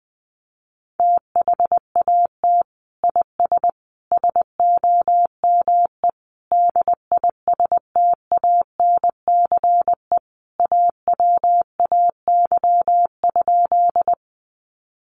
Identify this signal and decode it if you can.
Morse code, heard as a keyed tone, about 20 wpm, 700 Hz. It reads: THAT IS SOME DISTANCE AWAY?
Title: Sentences from Top 1200 Words 20wpm